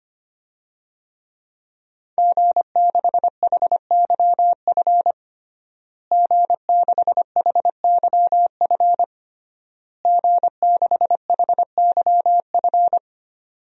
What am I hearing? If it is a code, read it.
Morse code, heard as a keyed tone, about 25 wpm, 700 Hz. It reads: G6HYF G6HYF G6HYF
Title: DXCC-100 Call Signs - 5 Characters - Repeated 3x - 25wpm